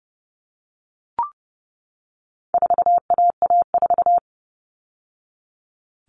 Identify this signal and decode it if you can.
Morse code, heard as a keyed tone, about 30 wpm, 700 Hz. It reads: 4AA4
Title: All Cut Numbers - 4-Digit 30wpm